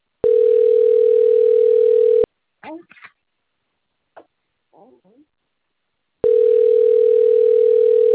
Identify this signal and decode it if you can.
Ringback tone (North America), 440 + 480 Hz, with speech.
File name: rg-1600-unknown- - - .17229